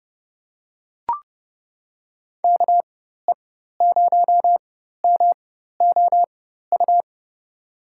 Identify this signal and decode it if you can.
Morse code, heard as a keyed tone, about 30 wpm, spaced as for 14 wpm, 700 Hz. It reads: KE0MOU
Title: USA Call Signs - 6 Characters - 4x Character Spacing - 30wpm